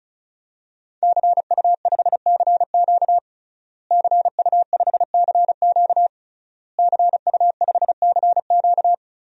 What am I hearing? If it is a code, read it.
Morse code, heard as a keyed tone, about 36 wpm, 700 Hz. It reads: CU5CQ CU5CQ CU5CQ
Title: DXCC-100 Call Signs - 5 Characters - Repeated 3x - 35wpm